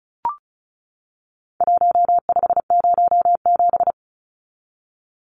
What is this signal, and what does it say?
Morse code, heard as a keyed tone, about 34 wpm, 700 Hz. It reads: 1507
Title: Numbers - Contest Style - 4-Digit 35wpm